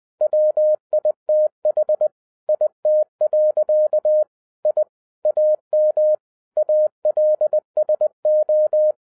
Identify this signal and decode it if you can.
Morse code, heard as a keyed tone, about 20 wpm, 610 Hz. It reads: WITH IT. I AM ALSO